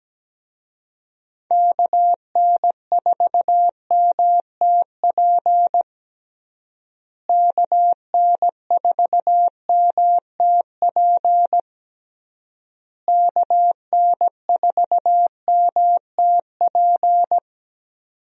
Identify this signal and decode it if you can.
Morse code, heard as a keyed tone, about 17 wpm, 700 Hz. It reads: KN4MTP KN4MTP KN4MTP
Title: USA Call Signs - 6 Characters - Repeated 3x - 17wpm